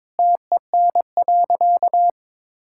Morse code, heard as a keyed tone, about 22 wpm, 700 Hz. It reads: TEN.
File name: practice / 22wpm / Easy Sentences with Hints - 22wpm